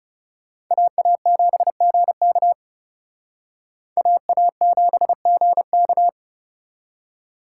Morse code, heard as a keyed tone, about 38 wpm, 700 Hz. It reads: AA7GK AA7GK